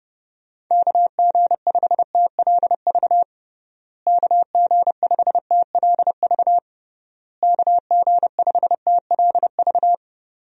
Morse code, heard as a keyed tone, about 30 wpm, 700 Hz. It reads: KG5TLV KG5TLV KG5TLV
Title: USA Call Signs - 6 Characters - Repeated 3x - 30wpm